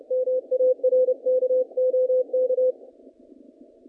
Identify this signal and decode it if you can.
Morse code, heard as a keyed tone, about 29 wpm, 510 Hz. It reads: MARK OK
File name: contests / naqp-cw-k5tr-op-k5pi- -0100